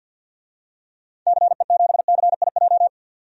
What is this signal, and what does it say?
Morse code, heard as a keyed tone, about 53 wpm, 700 Hz. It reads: CE7CIJ
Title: DXCC-100 Call Signs - 6 Characters - 50wpm